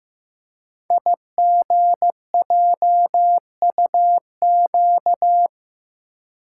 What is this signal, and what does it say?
Morse code, heard as a keyed tone, about 15 wpm, 700 Hz. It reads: IGJUQ
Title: Code Groups - 5 Letters - 15wpm